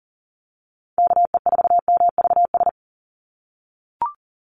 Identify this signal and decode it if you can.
Morse code, heard as a keyed tone, about 40 wpm, 700 Hz. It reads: KE4MVS
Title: USA Call Signs 40wpm